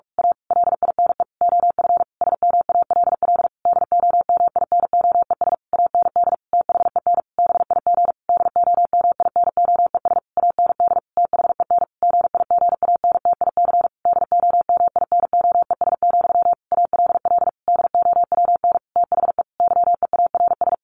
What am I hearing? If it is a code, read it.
Morse code, heard as a keyed tone, about 44 wpm, 700 Hz. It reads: A LINE OF SMALL DOMINOES AND THEN BIG DOMINOES AND THEN GIGANTIC DOMINOES, ALL DOWN THE YEARS
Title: A Sound Of Thunder 45wpm